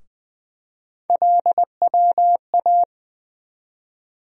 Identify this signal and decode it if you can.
Morse code, heard as a keyed tone, about 21 wpm, 700 Hz. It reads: LWA
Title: USA Call Sign Postfixes - 3 Letters 20wpm